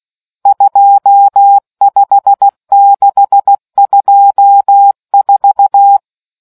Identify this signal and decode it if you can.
Morse code, heard as a keyed tone, about 16 wpm, 800 Hz. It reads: 25624